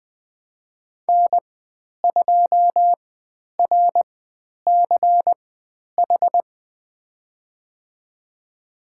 Morse code, heard as a keyed tone, about 20 wpm, spaced as for 10 wpm, 700 Hz. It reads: N2RCH
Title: 095 Lesson 16g - Letter c - Call Signs 20wpm